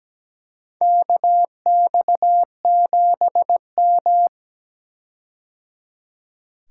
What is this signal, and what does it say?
Morse code, heard as a keyed tone, about 17 wpm, 700 Hz. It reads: KX7M
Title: K1USN SST QSO Practice - 17wpm